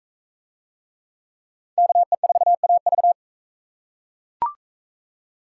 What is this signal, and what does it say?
Morse code, heard as a keyed tone, about 38 wpm, 700 Hz. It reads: KE4AV